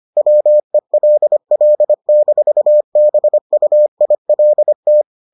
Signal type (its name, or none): Morse code